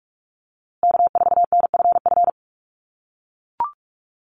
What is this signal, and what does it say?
Morse code, heard as a keyed tone, about 44 wpm, 700 Hz. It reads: K4NFF
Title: USA Call Signs Repeated 2x 45wpm